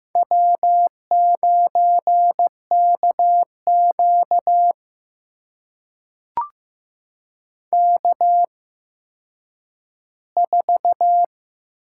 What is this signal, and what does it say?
Morse code, heard as a keyed tone, about 15 wpm, spaced as for 9 wpm, 700 Hz. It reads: W9KQ K4
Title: USA Call Signs - 4 Characters - 8x Character Spacing - 15wpm